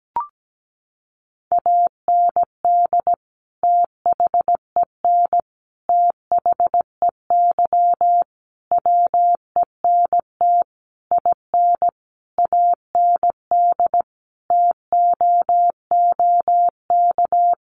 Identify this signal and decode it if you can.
Morse code, heard as a keyed tone, about 17 wpm, 700 Hz. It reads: AND THEN THEY WENT IN AND TOOK
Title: Sentences from Top 1400 Words 17wpm